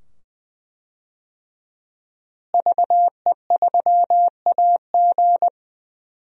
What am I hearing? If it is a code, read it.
Morse code, heard as a keyed tone, about 20 wpm, 700 Hz. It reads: VE3AG
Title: DXCC-100 Call Signs - 5 Characters - Repeated 2x - 20wpm